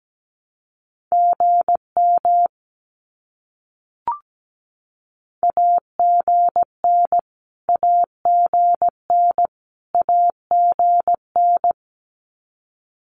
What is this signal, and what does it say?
Morse code, heard as a keyed tone, about 17 wpm, 700 Hz. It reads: GM AGN AGN AGN
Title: Abbreviations Repeated 3x - 17wpm